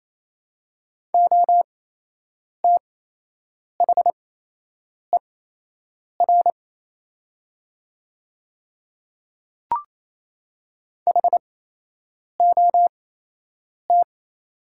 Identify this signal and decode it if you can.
Morse code, heard as a keyed tone, about 27 wpm, spaced as for 8 wpm, 690 Hz. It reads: OTHER HOT
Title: 8x Character Spacing - Top 200 Words - 28wpm